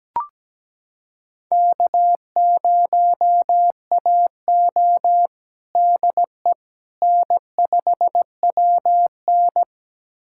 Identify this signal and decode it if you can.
Morse code, heard as a keyed tone, about 17 wpm, 700 Hz. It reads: K0AO DE N5WN
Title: USA Callsign DE Callsign - 4 Characters - 17wpm